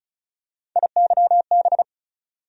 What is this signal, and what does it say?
Morse code, heard as a keyed tone, about 36 wpm, 700 Hz. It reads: IYB
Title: USA Call Sign Postfixes - 3 Letters 35wpm